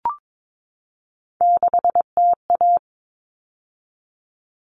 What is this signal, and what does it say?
Morse code, heard as a keyed tone, about 22 wpm, 700 Hz. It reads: 6TA
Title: All Cut Numbers - 3-Digit 22wpm